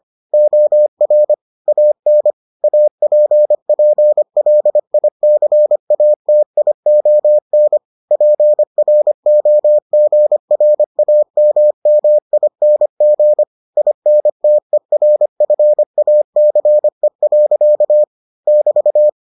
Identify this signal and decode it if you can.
Morse code, heard as a keyed tone, about 25 wpm, 600 Hz. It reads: OR AN APPLICATION PROGRAMMING INTERFACE. <BT>